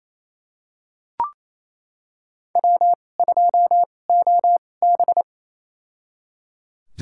Morse code, heard as a keyed tone, about 27 wpm, spaced as for 20 wpm, 700 Hz. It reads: W2OB
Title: USA Call Signs - 4 Characters - 2x Character Spacing - 28wpm